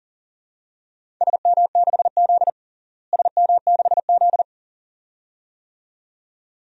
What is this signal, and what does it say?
Morse code, heard as a keyed tone, about 37 wpm, 700 Hz. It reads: SM6Z SM6Z
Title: DXCC-100 Call Signs - 4 Characters - Repeated 2x - 40wpm